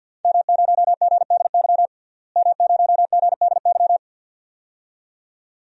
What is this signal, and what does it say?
Morse code, heard as a keyed tone, about 48 wpm, 700 Hz. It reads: M0GDY M0GDY
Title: DXCC-100 Call Signs - 5 Characters - Repeated 2x - 50wpm